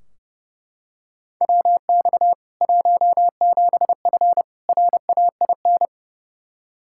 Morse code, heard as a keyed tone, about 30 wpm, 700 Hz. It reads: WX 17F RAIN
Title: K1USN SST QSO Practice 30wpm